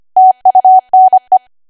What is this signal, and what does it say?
Morse code, heard as a keyed tone, about 25 wpm, 750 Hz. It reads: TUNE
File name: Archive / 25 WPM / 25WPM